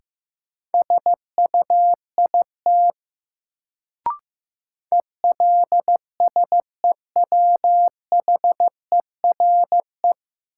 Morse code, heard as a keyed tone, about 15 wpm, 700 Hz. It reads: SUIT ELSEWHERE
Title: Sentences from Top 1600 Words - New Word Drill - 15wpm